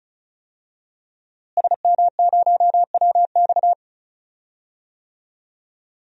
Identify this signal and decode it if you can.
Morse code, heard as a keyed tone, about 36 wpm, 700 Hz. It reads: SM0WX
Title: DXCC-100 Call Signs - 5 Characters - Repeated 2x - 35wpm